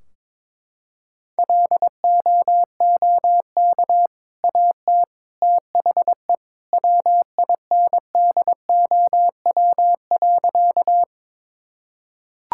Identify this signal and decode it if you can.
Morse code, heard as a keyed tone, about 22 wpm, 700 Hz. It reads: LOOK AT THE WINDOW.
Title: Sentences from Top 1600 Words 22wpm